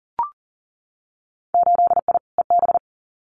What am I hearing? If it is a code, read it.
Morse code, heard as a keyed tone, about 38 wpm, 700 Hz. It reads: 8I EB
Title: Winter Field Day Exchanges Encore 40wpm